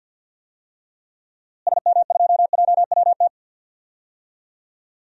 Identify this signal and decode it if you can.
Morse code, heard as a keyed tone, about 53 wpm, 700 Hz. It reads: SM2JWT
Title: DXCC-100 Call Signs - 6 Characters - Repeated 2x - 50wpm